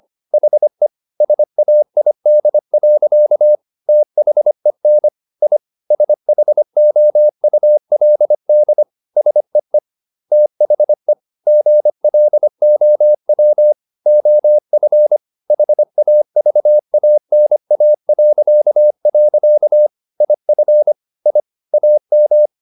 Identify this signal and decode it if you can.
Morse code, heard as a keyed tone, about 25 wpm, 600 Hz. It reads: HE SAID. THEN I SHOULD SEE THE GLOW OF HAVANA.. IF I AM